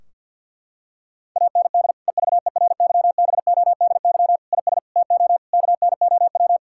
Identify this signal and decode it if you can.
Morse code, heard as a keyed tone, about 50 wpm, 700 Hz. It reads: AND EVERYBODY IS TO KNOW
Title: Sentences from Top 1600 Words 50wpm